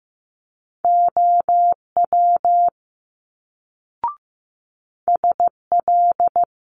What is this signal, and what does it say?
Morse code, heard as a keyed tone, about 15 wpm, 700 Hz. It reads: OW SL
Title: Multi-letter Phonemes Quick Fire 15wpm